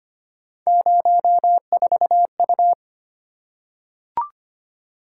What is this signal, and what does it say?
Morse code, heard as a keyed tone, about 25 wpm, 700 Hz. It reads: 04U
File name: Three Letter-Number Combinations - 25wpm